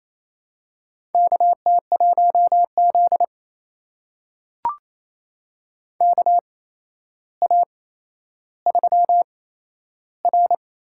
Morse code, heard as a keyed tone, about 28 wpm, spaced as for 12 wpm, 700 Hz. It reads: KT1Z KA3R
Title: USA Call Signs - 4 Characters - 8x Character Spacing - 28wpm